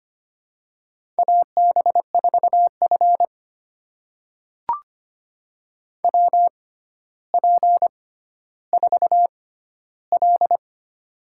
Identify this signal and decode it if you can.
Morse code, heard as a keyed tone, about 25 wpm, spaced as for 13 wpm, 700 Hz. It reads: AB4F WP4L
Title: USA Call Signs - 4 Characters - 6x Character Spacing - 25wpm